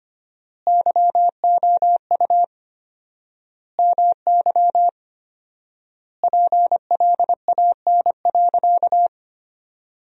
Morse code, heard as a keyed tone, about 25 wpm, 700 Hz. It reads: YOU MY PLAN.
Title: 4x Word-Spacing - Sentences from Top 500 Words - 25wpm